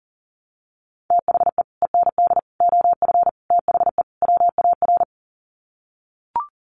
Morse code, heard as a keyed tone, about 40 wpm, 700 Hz. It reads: THE END OF THE WAR
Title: Sets of 5 Words 40wpm